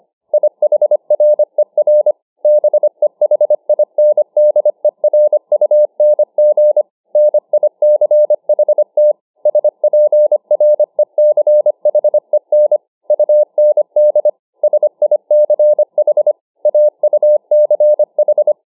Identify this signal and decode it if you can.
Morse code, heard as a keyed tone, about 25 wpm, 600 Hz. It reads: IHRER BEHINDERUNG NICHT SPRECHEN UND SICH AUCH